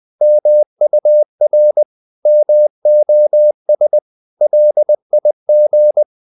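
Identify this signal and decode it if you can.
Morse code, heard as a keyed tone, about 20 wpm, 600 Hz. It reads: MUR MOS LIG
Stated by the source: Morse code practice of short Danish words at 20 wpm, from CW med Gnister.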